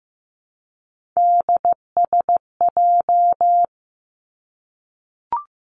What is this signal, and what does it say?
Morse code, heard as a keyed tone, about 15 wpm, 700 Hz. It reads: DSJ
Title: Three Challenging Letter-Number Combinations 15wpm